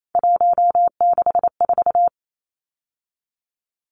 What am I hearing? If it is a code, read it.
Morse code, heard as a keyed tone, about 28 wpm, 700 Hz. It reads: 164
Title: Numbers - 3-Digit 28wpm